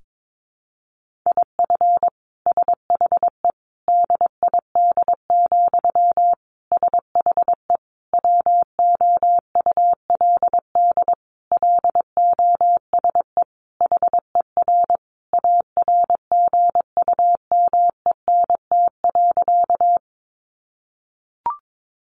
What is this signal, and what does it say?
Morse code, heard as a keyed tone, about 22 wpm, 700 Hz. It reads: IF SHE DID, SHE WOULD LOSE HER ARGUMENT.
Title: Sentences from Top 1400 Words 22wpm